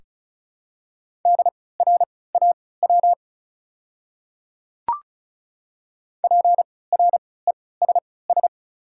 Morse code, heard as a keyed tone, about 34 wpm, spaced as for 20 wpm, 700 Hz. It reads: DRAW PRESS
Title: 3x Character Spacing - Top 300 Words - 35wpm